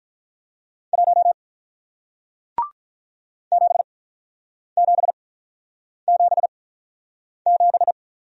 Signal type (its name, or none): Morse code